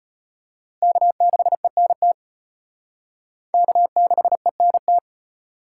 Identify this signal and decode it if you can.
Morse code, heard as a keyed tone, about 31 wpm, 700 Hz. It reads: K6ENT K6ENT